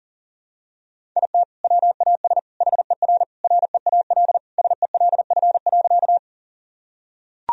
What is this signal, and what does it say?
Morse code, heard as a keyed tone, about 37 wpm, 700 Hz. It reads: IT WAS HER REAL SELF.
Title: Sentences from Top 1000 Words - Review Part 1 - 40wpm